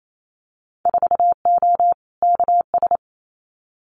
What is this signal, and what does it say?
Morse code, heard as a keyed tone, about 28 wpm, 700 Hz. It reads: 4O KS
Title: Winter Field Day Exchanges Encore 28wpm